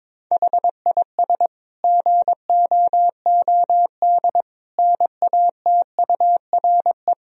Morse code, heard as a keyed tone, about 22 wpm, 700 Hz. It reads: HIS GOOD NATURE
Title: Sentences from Top 1000 Words - Review Part 1 - 22wpm